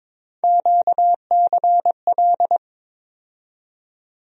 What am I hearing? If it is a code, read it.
Morse code, heard as a keyed tone, about 22 wpm, 700 Hz. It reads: QCL
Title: USA Call Sign Postfixes - 3 Letters 22wpm